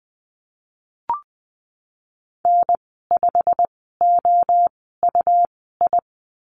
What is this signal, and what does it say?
Morse code, heard as a keyed tone, about 20 wpm, spaced as for 15 wpm, 700 Hz. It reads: N5OUI
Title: 128 Lesson 21g - Number 6 - Call Signs 20wpm